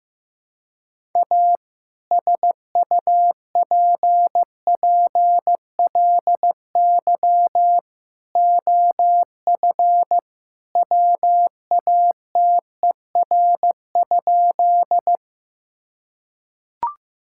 Morse code, heard as a keyed tone, about 15 wpm, 700 Hz. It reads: A SUPPLY OF WATER?
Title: Sentences from Top 1000 Words - Review Part 5 - 15wpm